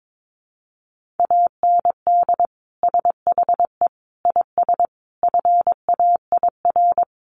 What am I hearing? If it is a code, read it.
Morse code, heard as a keyed tone, about 22 wpm, 700 Hz. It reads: AND SHE IS FAIR
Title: Sentences from Top 1300 Words 22wpm